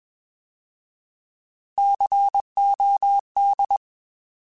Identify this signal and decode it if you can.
Morse code, heard as a keyed tone, about 21 wpm, 800 Hz. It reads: COD
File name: Alphabetical / Alphabetical mp3s / Cs @20